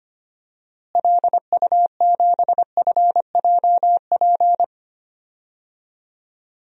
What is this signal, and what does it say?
Morse code, heard as a keyed tone, about 25 wpm, 700 Hz. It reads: LU7FJP
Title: DXCC-100 Call Signs - 6 Characters - Repeated 3x - 25wpm